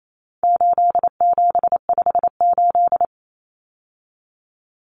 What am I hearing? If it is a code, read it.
Morse code, heard as a keyed tone, about 28 wpm, 700 Hz. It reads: 8758
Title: Numbers - 4-Digit 28wpm